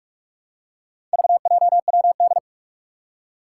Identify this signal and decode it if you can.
Morse code, heard as a keyed tone, about 44 wpm, 700 Hz. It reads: VJWD
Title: Four Challenging Letter-Number Combinations 45wpm